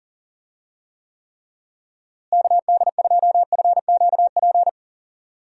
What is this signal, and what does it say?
Morse code, heard as a keyed tone, about 37 wpm, 700 Hz. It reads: KD2FQP
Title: USA Call Signs - 6 Characters - Repeated 3x - 40wpm